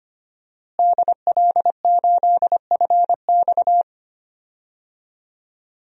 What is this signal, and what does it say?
Morse code, heard as a keyed tone, about 25 wpm, 700 Hz. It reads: DL8FX
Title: DXCC-100 Call Signs - 5 Characters - Repeated 3x - 25wpm